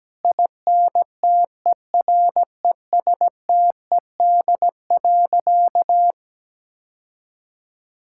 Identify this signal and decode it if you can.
Morse code, heard as a keyed tone, about 17 wpm, 700 Hz. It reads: INTERESTED.
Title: Sentences from Top 1000 Words - Review Part 6 - 17wpm